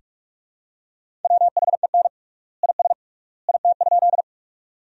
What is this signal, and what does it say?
Morse code, heard as a keyed tone, about 41 wpm, 700 Hz. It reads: WHEN IS IT?
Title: 3x Word-Spacing - Sentences from Top 100 Words - 45wpm